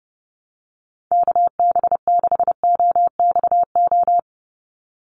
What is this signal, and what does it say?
Morse code, heard as a keyed tone, about 30 wpm, 700 Hz. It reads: KB6OXO
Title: USA Call Signs 30wpm